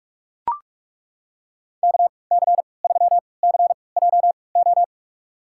Morse code, heard as a keyed tone, about 43 wpm, spaced as for 25 wpm, 700 Hz. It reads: KC3CJO
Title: USA Call Signs - 6 Characters - 3x Character Spacing - 45wpm